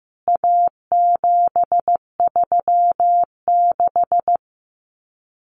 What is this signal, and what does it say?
Morse code, heard as a keyed tone, about 15 wpm, 700 Hz. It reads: A736
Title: CWT Exchange 15wpm